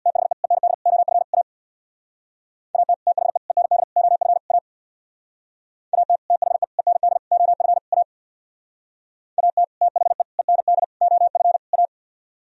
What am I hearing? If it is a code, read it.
Morse code, heard as a keyed tone, about 63 wpm, 700 Hz. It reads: THE END OF A AT THE END OF A AT THE END OF A AT THE END OF A